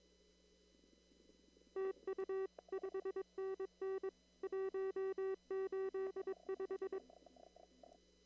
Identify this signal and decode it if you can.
Morse code, heard as a keyed tone, about 22 wpm, 380 Hz. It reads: TU 5NN 185
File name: CW / kt5j wpx cw-op-k5pi